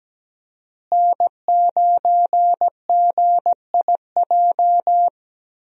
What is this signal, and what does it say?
Morse code, heard as a keyed tone, about 17 wpm, 700 Hz. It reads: N9GIJ
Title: USA Call Signs - 5 Characters - Repeated 2x - 17wpm